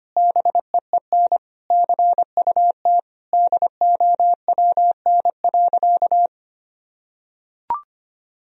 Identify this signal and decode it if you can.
Morse code, heard as a keyed tone, about 25 wpm, 700 Hz. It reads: BEEN CUT DOWN.
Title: Sentences from Top 1000 Words - Review Part 1 - 25wpm